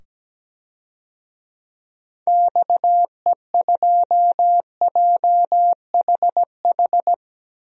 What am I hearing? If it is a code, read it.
Morse code, heard as a keyed tone, about 17 wpm, 700 Hz. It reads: XE2JHH
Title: DXCC-100 Call Signs - 6 Characters - 17wpm